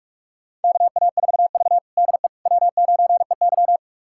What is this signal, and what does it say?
Morse code, heard as a keyed tone, about 46 wpm, 700 Hz. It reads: KA4V DE W9EY
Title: USA Callsign DE Callsign - 4 Characters - 45wpm